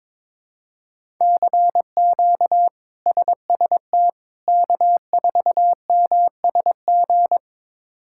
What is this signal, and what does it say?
Morse code, heard as a keyed tone, about 22 wpm, 700 Hz. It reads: CQ SST K4MSG
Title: K1USN SST QSO Practice - 22wpm